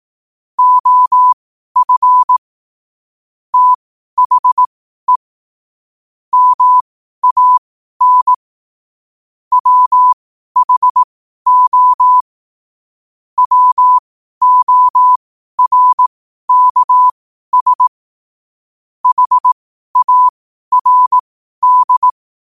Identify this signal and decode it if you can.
Morse code, heard as a keyed tone, about 18 wpm, spaced as for 12 wpm, 1000 Hz. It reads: OF THE MAN WHO WORKS HARD
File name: qotdfiles / 12wpm